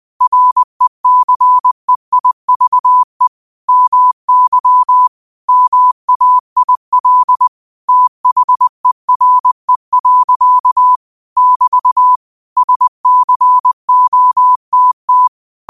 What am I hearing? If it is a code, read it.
Morse code, heard as a keyed tone, about 20 wpm, 1000 Hz. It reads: RECEIVE MY MAIL THERE. <BT> SCOTT
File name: 20wpm